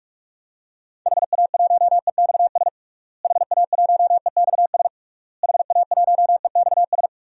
Morse code, heard as a keyed tone, about 46 wpm, 700 Hz. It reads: HA1EXS HA1EXS HA1EXS
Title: DXCC-100 Call Signs - 6 Characters - Repeated 3x - 45wpm